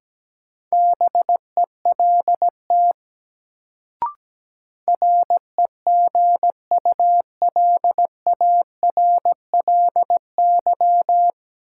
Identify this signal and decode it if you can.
Morse code, heard as a keyed tone, about 17 wpm, 700 Hz. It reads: BELT REGULARLY
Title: Sentences from Top 1400 Words - New Word Drill - 17wpm